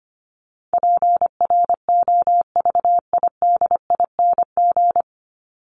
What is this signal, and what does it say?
Morse code, heard as a keyed tone, about 25 wpm, 700 Hz. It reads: PROVIDING
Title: ING Suffix - Top 100 Words with Base Words 25wpm